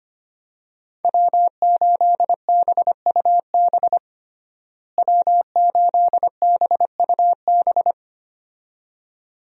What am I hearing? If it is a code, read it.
Morse code, heard as a keyed tone, about 25 wpm, 700 Hz. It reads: W8BUB W8BUB
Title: USA Call Signs - 5 Characters - Repeated 2x - 25wpm